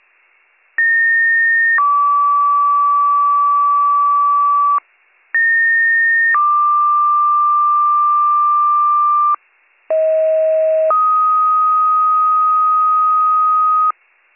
Two-tone page, heard as a one-second tone, then a three-second tone, about 1850 then 1150 Hz, about 1850 then 1180 Hz, about 640 then 1250 Hz. It is heard from 0.8 to 13.9 s.